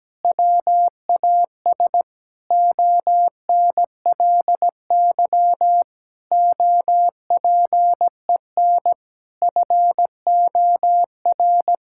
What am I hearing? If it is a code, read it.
Morse code, heard as a keyed tone, about 17 wpm, 700 Hz. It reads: WAS ONLY OPEN FOR
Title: Sentences from Top 1600 Words 17wpm